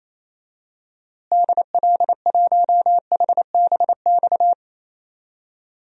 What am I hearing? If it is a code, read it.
Morse code, heard as a keyed tone, about 28 wpm, 700 Hz. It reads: DL1HBX
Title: DXCC-100 Call Signs - 6 Characters - Repeated 2x - 28wpm